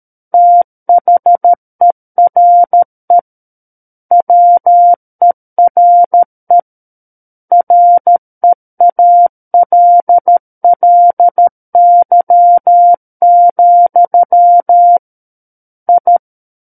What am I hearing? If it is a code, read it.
Morse code, heard as a keyed tone, about 13 wpm, 700 Hz. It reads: THERE WERE REALLY, I
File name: Book One / 04-Chapter 4 - 13 WPM